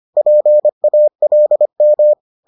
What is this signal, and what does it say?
Morse code, heard as a keyed tone, about 25 wpm, 600 Hz. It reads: PALM